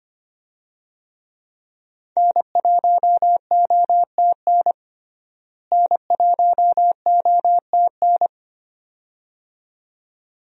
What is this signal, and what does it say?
Morse code, heard as a keyed tone, about 25 wpm, 700 Hz. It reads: N1OTN N1OTN
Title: USA Call Signs - 5 Characters - Repeated 2x - 25wpm